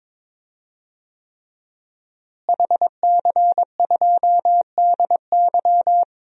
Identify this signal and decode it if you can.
Morse code, heard as a keyed tone, about 23 wpm, 700 Hz. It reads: HC2DY
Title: DXCC-100 Call Signs - 5 Characters - Repeated 3x - 22wpm